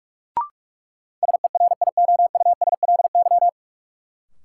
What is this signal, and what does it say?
Morse code, heard as a keyed tone, about 41 wpm, 700 Hz. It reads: SERIOUSLY
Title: Sentences from Top 1200 Words - New Word Drill Alt - 45wpm